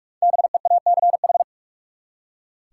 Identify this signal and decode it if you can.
Morse code, heard as a keyed tone, about 46 wpm, 700 Hz. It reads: BEACH
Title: Five Letter Words - 45wpm